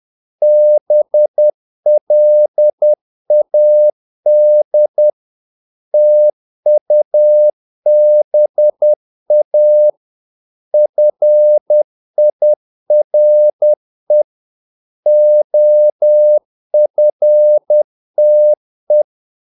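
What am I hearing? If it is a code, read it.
Morse code, heard as a keyed tone, about 10 wpm, 600 Hz. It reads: BLAD TUBA FIRE OFTE
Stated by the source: Morse practice of short Danish words at 10 wpm, from CW med Gnister.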